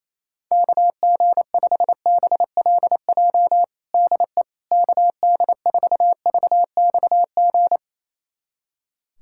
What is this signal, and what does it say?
Morse code, heard as a keyed tone, about 28 wpm, 700 Hz. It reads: KG5BLJ DE KD4VXG
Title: USA Callsign DE Callsign 28wpm